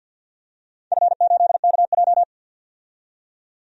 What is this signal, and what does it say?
Morse code, heard as a keyed tone, about 49 wpm, 700 Hz. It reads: F8KJ
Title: DXCC-100 Call Signs - 4 Characters - 50wpm